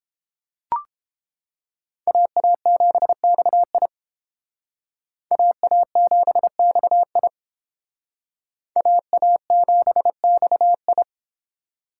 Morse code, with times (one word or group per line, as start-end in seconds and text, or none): 2.1-3.9 AA7XI
5.3-7.3 AA7XI
8.8-11.0 AA7XI